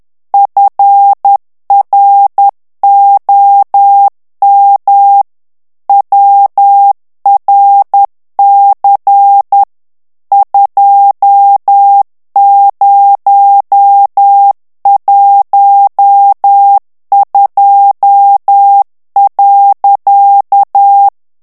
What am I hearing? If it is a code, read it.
Morse code, heard as a keyed tone, about 10 wpm, 800 Hz. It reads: FROM WRC 2012.
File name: ANGLAIS texte clair 3 vitesse 10 mots minute